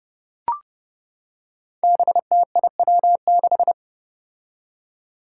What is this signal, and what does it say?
Morse code, heard as a keyed tone, about 30 wpm, 700 Hz. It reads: BTIW6
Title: Five Challenging Letter-Number Combinations 30wpm